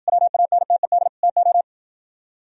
Morse code, heard as a keyed tone, about 63 wpm, 700 Hz. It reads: WANTED TO